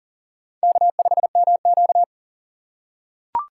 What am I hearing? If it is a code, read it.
Morse code, heard as a keyed tone, about 41 wpm, 700 Hz. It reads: K5MQ
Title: USA Call Signs - 3x Character Spacing - 40wpm